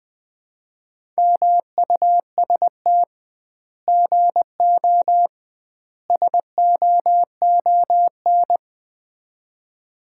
Morse code, heard as a keyed tone, about 20 wpm, 700 Hz. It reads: MUST GO SOON
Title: Common QSO Phrases - 2x Word-Spacing - 20wpm